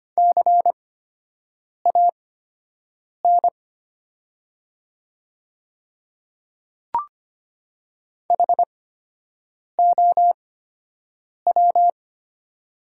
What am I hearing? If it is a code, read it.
Morse code, heard as a keyed tone, about 25 wpm, spaced as for 7 wpm, 700 Hz. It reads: CAN HOW
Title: 8x Character Spacing - Top 100 Words - 25wpm